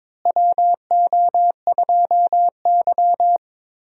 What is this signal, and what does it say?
Morse code, heard as a keyed tone, about 22 wpm, 700 Hz. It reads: WO2Y
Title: USA Call Signs - 4 Characters - Repeated 3x - 22wpm